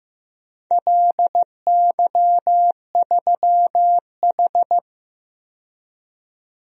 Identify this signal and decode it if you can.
Morse code, heard as a keyed tone, about 15 wpm, 700 Hz. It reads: LY3H
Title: DXCC-100 Call Signs - 4 Characters - 15wpm